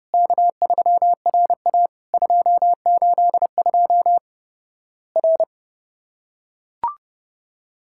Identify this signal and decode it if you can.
Morse code, heard as a keyed tone, about 30 wpm, 700 Hz. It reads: K3RA 282 R
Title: Worked All Europe DX Contest - QTC Exchange - 30wpm